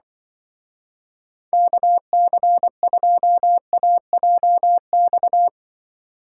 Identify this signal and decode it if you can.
Morse code, heard as a keyed tone, about 25 wpm, 700 Hz. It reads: KC2AJX
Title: Speed Racing - USA 2x3 Call Signs 20wpm